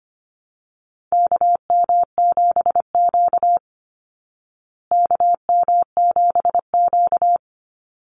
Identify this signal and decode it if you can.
Morse code, heard as a keyed tone, about 25 wpm, 700 Hz. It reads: KM7Q KM7Q
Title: Call Signs Repeated 2x - 25wpm